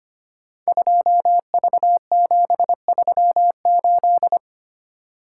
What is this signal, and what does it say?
Morse code, heard as a keyed tone, about 25 wpm, 700 Hz. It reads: 2V738
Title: Code Groups - 5 Letter:Numbers - 25wpm